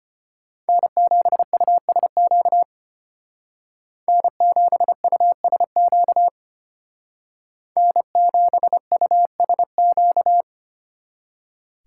Morse code, heard as a keyed tone, about 31 wpm, 700 Hz. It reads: N7USQ N7USQ N7USQ